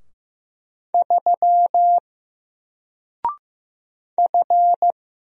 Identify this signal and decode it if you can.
Morse code, heard as a keyed tone, about 15 wpm, 700 Hz. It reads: 3 F
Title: Single Letter-Number 15wpm